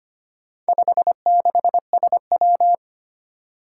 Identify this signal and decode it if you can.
Morse code, heard as a keyed tone, about 25 wpm, 700 Hz. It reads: 56SW
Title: Four Challenging Letter-Number Combinations 25wpm